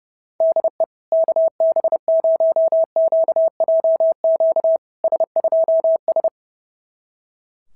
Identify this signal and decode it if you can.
Morse code, heard as a keyed tone, about 32 wpm, 650 Hz. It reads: DE KB0QJQ S2S
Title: SOTA QSO Practice 30wpm